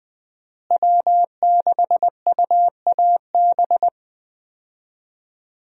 Morse code, heard as a keyed tone, about 20 wpm, 700 Hz. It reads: W6UAB
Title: USA Call Signs - 5 Characters - Repeated 2x - 20wpm